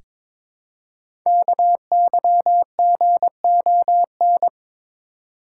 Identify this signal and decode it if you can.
Morse code, heard as a keyed tone, about 23 wpm, 700 Hz. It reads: KYGON
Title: Code Groups - 5 Letters - 22wpm